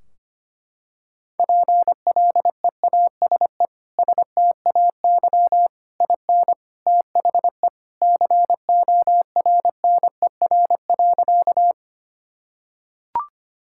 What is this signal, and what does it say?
Morse code, heard as a keyed tone, about 25 wpm, 700 Hz. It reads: PLEASE STAY IN THE CORNER.
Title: Sentences from Top 1600 Words 25wpm